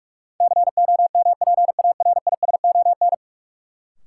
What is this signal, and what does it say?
Morse code, heard as a keyed tone, about 47 wpm, 700 Hz. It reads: COMPARISON
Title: Sentences from Top 1600 Words - New Word Drill - 45wpm